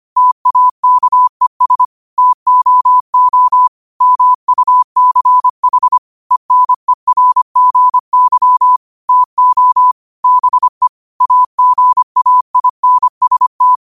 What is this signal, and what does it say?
Morse code, heard as a keyed tone, about 25 wpm, 1000 Hz. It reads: TAKES TOO MUCH ENERGY TO BE AGAINST
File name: qotdfiles / 25wpm